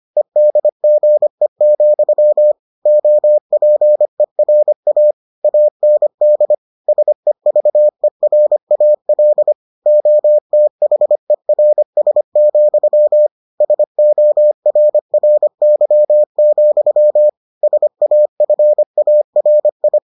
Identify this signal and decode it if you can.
Morse code, heard as a keyed tone, about 25 wpm, 600 Hz. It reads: EDGE, OPERA AND SEVERAL OTHERS, SORRY, SAFARI